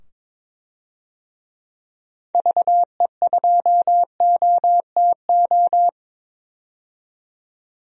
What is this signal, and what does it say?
Morse code, heard as a keyed tone, about 22 wpm, 700 Hz. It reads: VE2OTO
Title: DXCC-100 Call Signs - 6 Characters - 22wpm